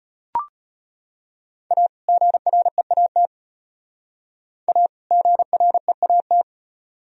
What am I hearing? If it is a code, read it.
Morse code, heard as a keyed tone, about 33 wpm, 700 Hz. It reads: A GREAT A GREAT